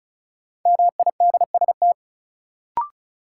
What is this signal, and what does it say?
Morse code, heard as a keyed tone, about 36 wpm, 700 Hz. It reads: MIDST
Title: Sets of 4 Words - New Word Drill - 35wpm